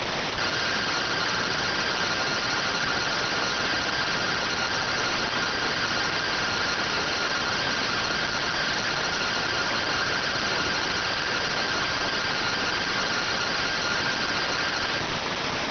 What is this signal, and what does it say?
Two-tone teleprinter signal, about 170 Hz shift.